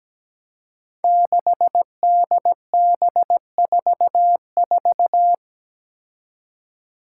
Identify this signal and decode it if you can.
Morse code, heard as a keyed tone, about 17 wpm, 700 Hz. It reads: 6DB44
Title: Five Challenging Letter-Number Combinations 17wpm